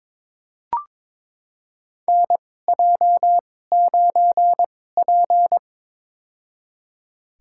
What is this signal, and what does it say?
Morse code, heard as a keyed tone, about 22 wpm, spaced as for 16 wpm, 700 Hz. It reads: NJ9P
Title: USA Call Signs - 4 Characters - 2x Character Spacing - 22wpm